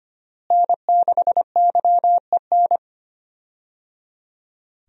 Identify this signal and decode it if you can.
Morse code, heard as a keyed tone, about 25 wpm, 700 Hz. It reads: N6YEN
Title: USA Call Signs - 5 Characters - Repeated 3x - 25wpm